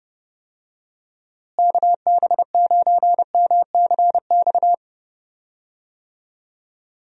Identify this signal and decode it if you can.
Morse code, heard as a keyed tone, about 30 wpm, 700 Hz. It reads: KB9MCX
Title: USA Call Signs - 6 Characters - Repeated 3x - 30wpm